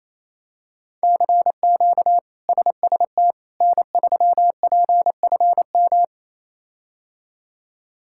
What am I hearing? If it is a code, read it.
Morse code, heard as a keyed tone, about 28 wpm, 700 Hz. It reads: CQ SST N3PFM
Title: K1USN SST QSO Practice - 28wpm